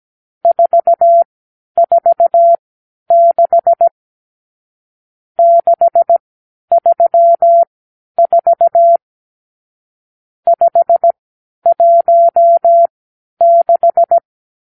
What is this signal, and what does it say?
Morse code, heard as a keyed tone, about 17 wpm, spaced as for 10 wpm, 680 Hz. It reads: 446 634 516
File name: k7qo / Lesson 50 (1-6)